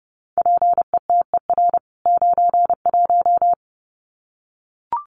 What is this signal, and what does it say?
Morse code, heard as a keyed tone, about 30 wpm, 700 Hz. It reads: PETER 91
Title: CWT Exchange 30wpm